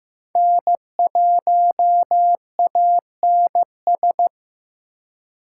Morse code, heard as a keyed tone, about 18 wpm, 700 Hz. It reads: N1ANS
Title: Speed Racing - USA 1x3 Call Signs 15wpm